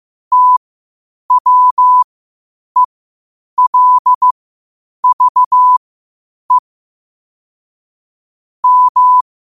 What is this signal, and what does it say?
Morse code, heard as a keyed tone, about 15 wpm, spaced as for 8 wpm, 1000 Hz. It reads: TWELVE M
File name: 8wpm